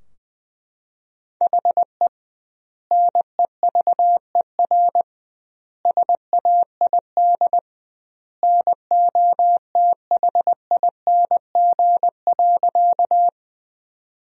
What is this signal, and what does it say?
Morse code, heard as a keyed tone, about 21 wpm, 700 Hz. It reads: HE NEVER SAID NOTHING.
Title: 2x Word-Spacing - Sentences from Top 500 Words - 20wpm